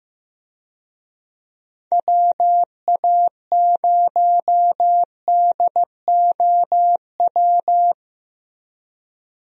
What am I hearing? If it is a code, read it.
Morse code, heard as a keyed tone, about 15 wpm, 700 Hz. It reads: WA0DOW
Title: USA Call Signs - 6 Characters - Repeated 2x - 15wpm